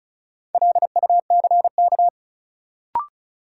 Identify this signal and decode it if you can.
Morse code, heard as a keyed tone, about 34 wpm, 700 Hz. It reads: LUCK
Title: Sentences from Top 1500 Words - New Word Drill Alt - 35wpm